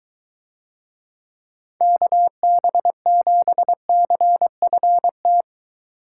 Morse code, heard as a keyed tone, about 22 wpm, 700 Hz. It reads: KB7CFT